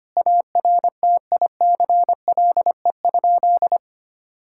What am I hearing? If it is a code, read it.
Morse code, heard as a keyed tone, about 25 wpm, 700 Hz. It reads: ARTICLE?
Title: Sentences from Top 1000 Words - Review Part 5 - 25wpm